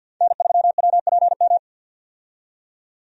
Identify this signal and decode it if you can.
Morse code, heard as a keyed tone, about 52 wpm, 700 Hz. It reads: N3WPM
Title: USA Call Signs - 5 Characters - 50wpm